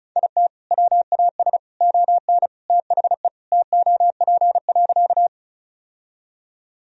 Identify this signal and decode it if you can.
Morse code, heard as a keyed tone, about 36 wpm, 700 Hz. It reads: IT WAS ON THE TOP.
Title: Sentences from Top 1000 Words - Review Part 2 - 35wpm